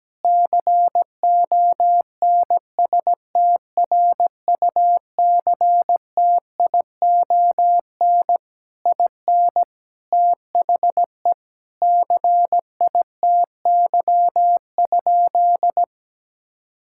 Morse code, heard as a keyed tone, about 17 wpm, 700 Hz. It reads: CONSTRUCTION IN THE CITY?
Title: Sentences from Top 1600 Words 17wpm